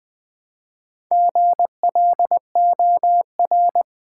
Morse code, heard as a keyed tone, about 20 wpm, 700 Hz. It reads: GLOR
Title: Build A Word - 2 Letters At A Time - 8-Letter Words - 20wpm